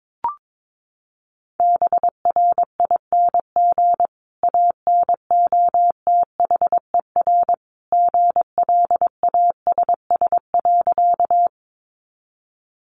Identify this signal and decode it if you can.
Morse code, heard as a keyed tone, about 22 wpm, 700 Hz. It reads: BRING ANOTHER GLASS.
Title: Sentences from Top 1300 Words 22wpm